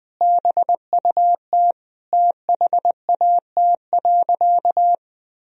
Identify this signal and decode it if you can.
Morse code, heard as a keyed tone, about 21 wpm, 700 Hz. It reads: BUT THAT.
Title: Sentences from Top 1000 Words - Review Part 3 - 20wpm